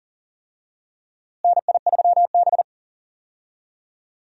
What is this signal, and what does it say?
Morse code, heard as a keyed tone, about 40 wpm, 700 Hz. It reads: NI3B
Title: USA Call Signs - 4 Characters - Repeated 2x - 40wpm